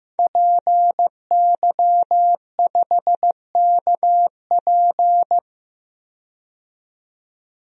Morse code, heard as a keyed tone, about 15 wpm, 700 Hz. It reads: PY5KP
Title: DXCC-100 Call Signs - 5 Characters - 15wpm